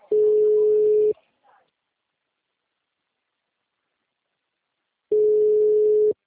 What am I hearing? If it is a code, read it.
Ringback tone (Europe, 425 Hz).